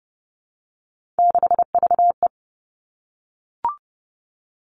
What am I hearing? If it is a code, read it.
Morse code, heard as a keyed tone, about 30 wpm, 700 Hz. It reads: 6VE
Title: Three Challenging Letter-Number Combinations 30wpm